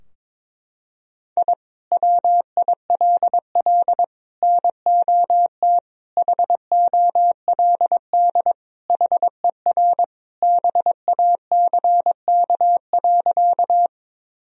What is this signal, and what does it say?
Morse code, heard as a keyed tone, about 22 wpm, 700 Hz. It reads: I WILL NOT HOLD HER BACK.
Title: Sentences from Top 1000 Words - Review Part 2 - 22wpm